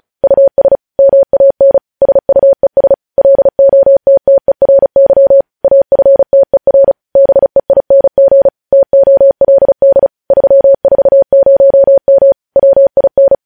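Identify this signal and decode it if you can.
Morse code, heard as a keyed tone, about 36 wpm, 550 Hz. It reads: US MAN SUES LOTTERY AFTER BEING TOLD 340M WIN
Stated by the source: Morse code news headlines sent at 35, beginning with vvv.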